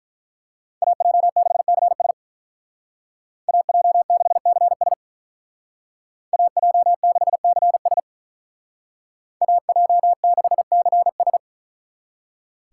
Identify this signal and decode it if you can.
Morse code, heard as a keyed tone, about 44 wpm, 700 Hz. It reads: AJ6CS AJ6CS AJ6CS AJ6CS